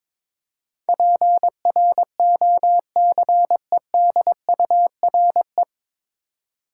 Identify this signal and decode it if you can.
Morse code, heard as a keyed tone, about 22 wpm, 700 Hz. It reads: PROCEDURE
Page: Morse Code Ninja practice 22wpm